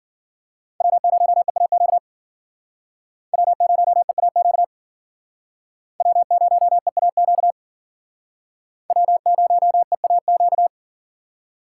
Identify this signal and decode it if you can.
Morse code, heard as a keyed tone, about 51 wpm, 700 Hz. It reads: W0EAQ W0EAQ W0EAQ W0EAQ